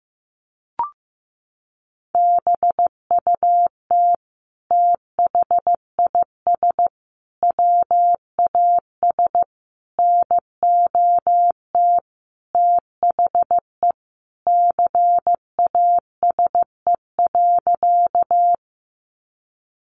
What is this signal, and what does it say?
Morse code, heard as a keyed tone, about 14 wpm, 700 Hz. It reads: BUT THIS WAS NOT THE CASE.
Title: Sentences from Top 700 Words 15wpm